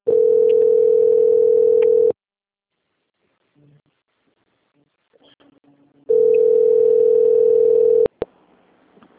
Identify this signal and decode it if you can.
Ringback tone (North America), 440 + 480 Hz.
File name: out- - - - - .22751